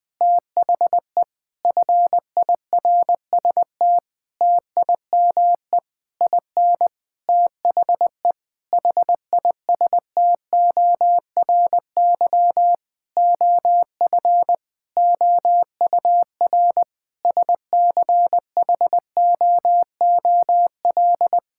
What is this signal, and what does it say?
Morse code, heard as a keyed tone, about 20 wpm, 700 Hz. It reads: THE FIRST TIME IN THE HISTORY OF OUR SCHOOL
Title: Sentences with 8-Word Phrases - 20wpm